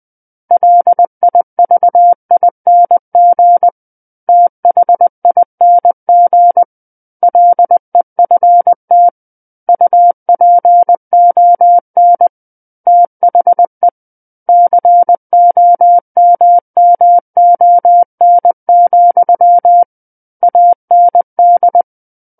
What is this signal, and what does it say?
Morse code, heard as a keyed tone, about 20 wpm, 700 Hz. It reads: LIVING THING LEFT UPON THE COMMON, AND
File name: 20 WPM